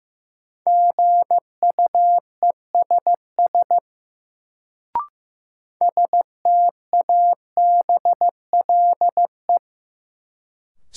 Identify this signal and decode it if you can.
Morse code, heard as a keyed tone, about 15 wpm, 700 Hz. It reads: GUESS STABLE
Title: Sentences from Top 1500 Words - New Word Drill - 15wpm